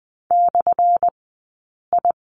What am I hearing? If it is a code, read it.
Morse code, heard as a keyed tone, about 20 wpm, spaced as for 8 wpm, 700 Hz. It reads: /I
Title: 217 Lesson 35d - Number 0 - Two Characters 20wpm